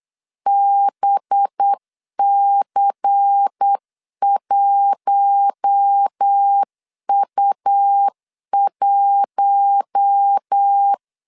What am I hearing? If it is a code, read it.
Morse code, heard as a keyed tone, about 8 wpm, 800 Hz. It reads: BC1U1
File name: CW / DET / 1 6 1500C S C 8wpm 5219